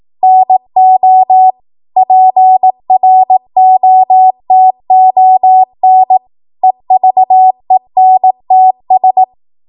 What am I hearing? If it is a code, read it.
Morse code, heard as a keyed tone, about 18 wpm, 750 Hz. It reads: NO PROTON EVENTS
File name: Morse / 18WPM